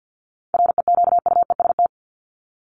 Morse code, heard as a keyed tone, about 50 wpm, 700 Hz. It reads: REQUEST
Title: Top 1751-2000 Words 50wpm